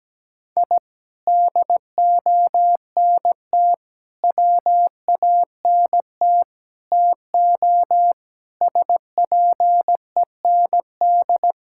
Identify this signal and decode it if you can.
Morse code, heard as a keyed tone, about 17 wpm, 700 Hz. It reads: I DONT WANT TO SPEND
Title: Sentences from Top 1000 Words - Review Part 5 - 17wpm